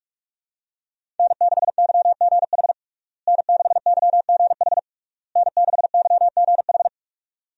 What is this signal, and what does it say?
Morse code, heard as a keyed tone, about 46 wpm, 700 Hz. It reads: N6YGH N6YGH N6YGH
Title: USA Call Signs - 5 Characters - Repeated 3x - 45wpm